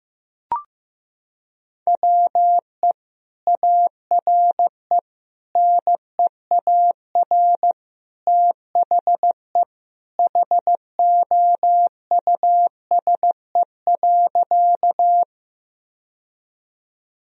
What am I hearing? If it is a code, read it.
Morse code, heard as a keyed tone, about 16 wpm, 700 Hz. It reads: WE ARE NEAR THE HOUSE.
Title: Sentences from Top 1000 Words - Review Part 1 - 15wpm